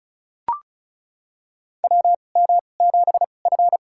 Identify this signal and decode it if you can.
Morse code, heard as a keyed tone, about 36 wpm, spaced as for 25 wpm, 700 Hz. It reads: WM7F
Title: USA Call Signs - 2x Character Spacing - 35wpm